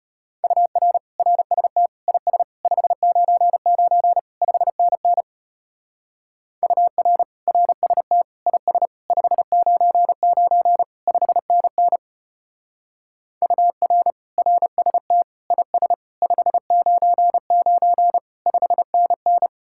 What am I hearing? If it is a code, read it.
Morse code, heard as a keyed tone, about 31 wpm, 700 Hz. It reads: UR RST IS 599 5NN UR RST IS 599 5NN UR RST IS 599 5NN
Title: Speed-Racing Common QSO Phrases Encore - 25wpm